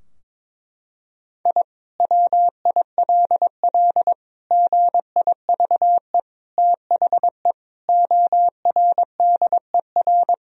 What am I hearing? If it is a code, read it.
Morse code, heard as a keyed tone, about 22 wpm, 700 Hz. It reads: I WILL GIVE THE ORDER
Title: Sentences from Top 1000 Words - Review Part 2 - 22wpm